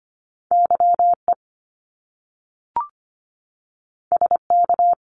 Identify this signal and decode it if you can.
Morse code, heard as a keyed tone, about 25 wpm, 700 Hz. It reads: YE SK
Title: Top Two Letter Combinations in Words 25wpm